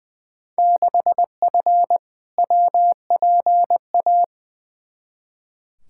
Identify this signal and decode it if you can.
Morse code, heard as a keyed tone, about 20 wpm, 700 Hz. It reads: 6F WPA
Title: Field Day Exchanges Encore - 20wpm